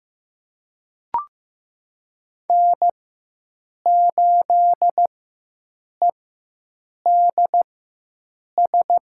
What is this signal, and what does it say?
Morse code, heard as a keyed tone, about 15 wpm, spaced as for 7 wpm, 700 Hz. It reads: N8EDS
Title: USA Call Signs - 4x Character Spacing - 15wpm